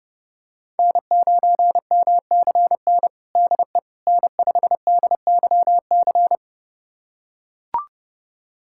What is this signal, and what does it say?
Morse code, heard as a keyed tone, about 30 wpm, 700 Hz. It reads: N9MCN DE N5DYC
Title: USA Callsign DE Callsign - 5 Characters - 30wpm